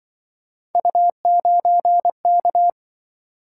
Morse code, heard as a keyed tone, about 24 wpm, 700 Hz. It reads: U9K
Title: Speed-Racing - Three Letter-Number Combinations - 20wpm